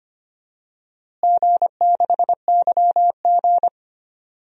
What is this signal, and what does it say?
Morse code, heard as a keyed tone, about 25 wpm, 700 Hz. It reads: G6YG
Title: DXCC-100 Call Signs - 4 Characters - Repeated 2x - 25wpm